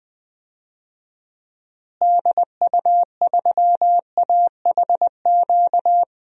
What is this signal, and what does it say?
Morse code, heard as a keyed tone, about 21 wpm, 700 Hz. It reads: DU3AHQ
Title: DXCC-100 Call Signs - 6 Characters - Repeated 2x - 20wpm